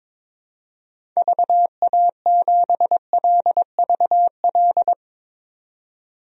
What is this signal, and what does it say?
Morse code, heard as a keyed tone, about 22 wpm, 700 Hz. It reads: VA7LVL
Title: DXCC-100 Call Signs - 6 Characters - 22wpm